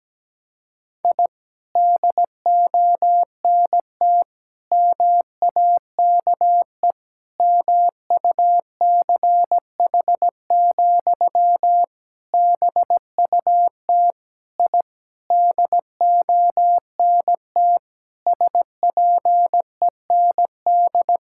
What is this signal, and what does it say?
Morse code, heard as a keyed tone, about 17 wpm, 700 Hz. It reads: I DONT MAKE MUCH, BUT I DONT SPEND
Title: Sentences from Top 1000 Words - Review Part 5 - 17wpm